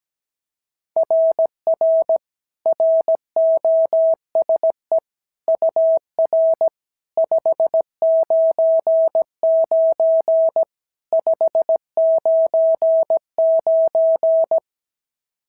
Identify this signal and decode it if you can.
Morse code, heard as a keyed tone, about 16 wpm, 650 Hz. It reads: RR ROSE UR 599 599
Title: SOTA QSO Practice 17wpm